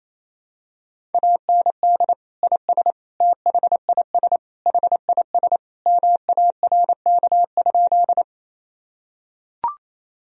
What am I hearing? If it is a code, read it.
Morse code, heard as a keyed tone, about 27 wpm, 700 Hz. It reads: AND IS THIS HIS MARK?
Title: Sentences from Top 1000 Words - Review Part 1 - 28wpm